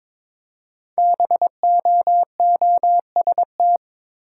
Morse code, heard as a keyed tone, about 22 wpm, 700 Hz. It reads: BOOST
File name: 5 Letter Words - First 3 Letters Given - 22wpm